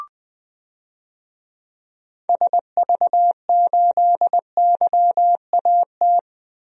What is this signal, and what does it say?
Morse code, heard as a keyed tone, about 21 wpm, 700 Hz. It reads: SV8YAT
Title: DXCC-100 Call Signs - 6 Characters - Repeated 2x - 20wpm